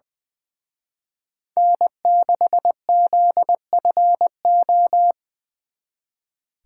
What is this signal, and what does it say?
Morse code, heard as a keyed tone, about 24 wpm, 700 Hz. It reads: N6ZFO
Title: Speed Racing - USA 1x3 Call Signs 20wpm